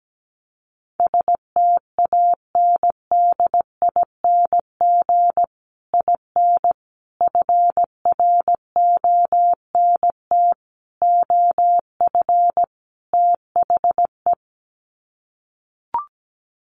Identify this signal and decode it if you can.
Morse code, heard as a keyed tone, about 17 wpm, 700 Hz. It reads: STANDING IN FRONT OF THE
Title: Sets of 5 Words Encore2 17wpm